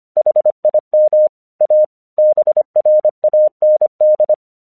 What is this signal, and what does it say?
Morse code, heard as a keyed tone, about 25 wpm, 610 Hz. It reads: HIM A BRAND
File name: 1 English 25wpm / AQC14